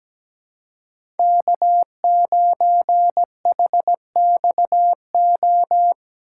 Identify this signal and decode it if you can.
Morse code, heard as a keyed tone, about 17 wpm, 700 Hz. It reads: K9HXO
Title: USA Call Signs - 5 Characters - 8x Character Spacing - 17wpm